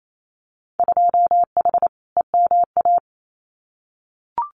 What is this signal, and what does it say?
Morse code, heard as a keyed tone, about 28 wpm, 700 Hz. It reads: 2H EMA
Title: Winter Field Day Exchanges 28wpm